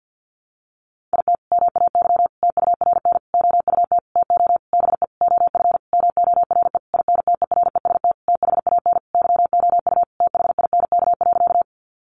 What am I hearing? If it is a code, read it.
Morse code, heard as a keyed tone, about 50 wpm, 700 Hz. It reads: IT MAY TURN OUT TO BE OF MORE INTEREST THAN YOU THINK.
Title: Sentences from Top 400 Words 50wpm